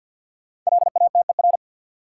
Morse code, heard as a keyed tone, about 53 wpm, 700 Hz. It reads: LATER
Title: Five Letter Words - 50wpm